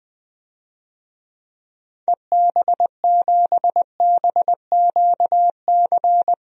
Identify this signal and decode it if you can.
Morse code, heard as a keyed tone, about 20 wpm, 700 Hz. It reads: EB7BQC
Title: DXCC-100 Call Signs - 6 Characters - Repeated 2x - 20wpm